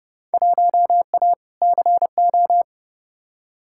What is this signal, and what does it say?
Morse code, heard as a keyed tone, about 30 wpm, 700 Hz. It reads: 1A CO
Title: Field Day Exchanges - 30wpm